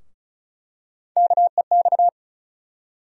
Morse code, heard as a keyed tone, about 36 wpm, 700 Hz. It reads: KEX
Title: USA Call Sign Postfixes - 3 Letters 35wpm